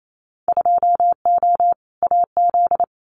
Morse code, heard as a keyed tone, about 28 wpm, 700 Hz. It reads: 2O AZ
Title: Winter Field Day Exchanges 28wpm